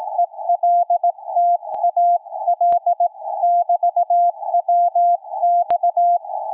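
Morse code, heard as a keyed tone, about 18 wpm, 700 Hz, with radio noise. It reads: ELTA L <BT> WX
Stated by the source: QRP station's CW heard around 7029.2 kHz; signal fading from a solid 559 down to complete dropout.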